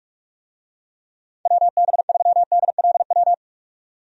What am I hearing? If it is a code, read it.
Morse code, heard as a keyed tone, about 46 wpm, 700 Hz. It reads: WB3DLW
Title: USA Call Signs - 6 Characters - 45wpm